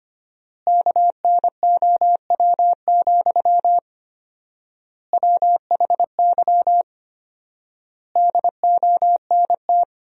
Morse code, heard as a keyed tone, about 25 wpm, 700 Hz. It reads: KNOW, WHY DONT
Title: 4x Word-Spacing - Sentences from Top 300 Words - 25wpm